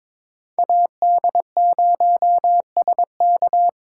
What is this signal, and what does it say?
Morse code, heard as a keyed tone, about 22 wpm, 700 Hz. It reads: AD0SK
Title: USA Call Signs - 2x Character Spacing - 22wpm